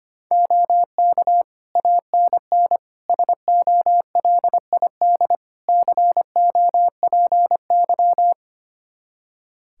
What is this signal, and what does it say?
Morse code, heard as a keyed tone, about 25 wpm, 700 Hz. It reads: OK ANN SOLID COPY
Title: K1USN SST QSO Practice 25wpm